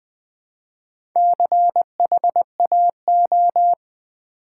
Morse code, heard as a keyed tone, about 20 wpm, 700 Hz. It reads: CHAO
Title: Build A Word - 1 Letter At A Time - 5-Letter Words - 20wpm